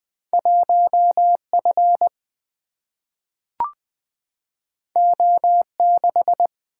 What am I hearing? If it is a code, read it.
Morse code, heard as a keyed tone, about 20 wpm, 700 Hz. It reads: 1F O6
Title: Single-Letter Single-Number 20wpm